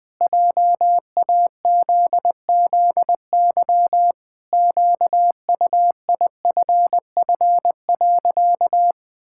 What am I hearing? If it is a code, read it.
Morse code, heard as a keyed tone, about 20 wpm, 700 Hz. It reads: JAZZY QUIFF.
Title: Pangram Challenge v2 - 20wpm